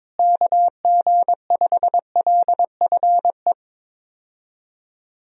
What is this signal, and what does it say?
Morse code, heard as a keyed tone, about 21 wpm, 700 Hz. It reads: KG5LFE
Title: USA Call Signs - 6 Characters - Repeated 3x - 22wpm